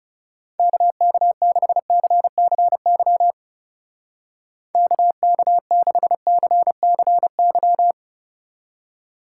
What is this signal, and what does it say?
Morse code, heard as a keyed tone, about 37 wpm, 700 Hz. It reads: KK6CCY KK6CCY
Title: Speed Racing - USA 2x3 Call Signs 30wpm